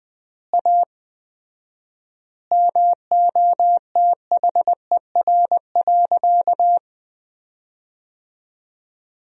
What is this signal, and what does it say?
Morse code, heard as a keyed tone, about 20 wpm, 700 Hz. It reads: A MOTHER.
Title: 4x Word-Spacing - Sentences from Top 200 Words - 20wpm